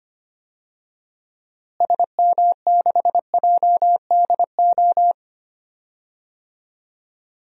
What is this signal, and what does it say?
Morse code, heard as a keyed tone, about 25 wpm, 700 Hz. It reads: SM6JDO
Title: DXCC-100 Call Signs - 6 Characters - Repeated 3x - 25wpm